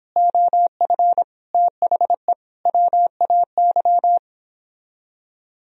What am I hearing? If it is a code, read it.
Morse code, heard as a keyed tone, about 29 wpm, 700 Hz. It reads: OF THE WAY